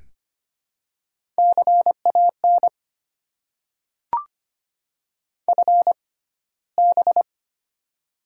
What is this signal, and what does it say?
Morse code, heard as a keyed tone, about 25 wpm, spaced as for 15 wpm, 700 Hz. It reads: CAN FB
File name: Top 90 QSO Elements - 6x Character Spacing - 25wpm